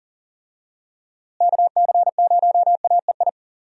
Morse code, heard as a keyed tone, about 40 wpm, 700 Hz. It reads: KC0AEI
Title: USA Call Signs - 6 Characters - Repeated 2x - 40wpm